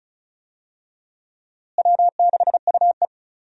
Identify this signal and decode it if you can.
Morse code, heard as a keyed tone, about 34 wpm, 700 Hz. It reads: W6UE
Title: USA Call Signs - 4 Characters - Repeated 3x - 35wpm